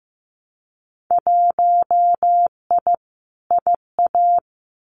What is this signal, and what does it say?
Morse code, heard as a keyed tone, about 15 wpm, 700 Hz. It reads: 1I IA
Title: Winter Field Day Exchanges 15wpm